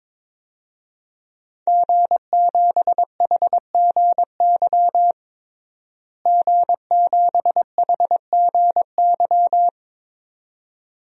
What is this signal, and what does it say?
Morse code, heard as a keyed tone, about 22 wpm, 700 Hz. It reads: G7HGY G7HGY
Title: DXCC-100 Call Signs - 5 Characters - Repeated 2x - 22wpm